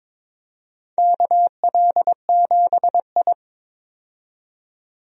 Morse code, heard as a keyed tone, about 22 wpm, 700 Hz. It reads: KL7I
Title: DXCC-100 Call Signs - 4 Characters - 22wpm